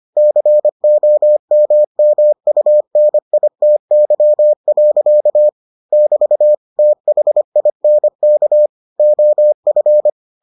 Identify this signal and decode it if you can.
Morse code, heard as a keyed tone, about 25 wpm, 600 Hz. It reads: COMMUNITY. <BT> THINK OF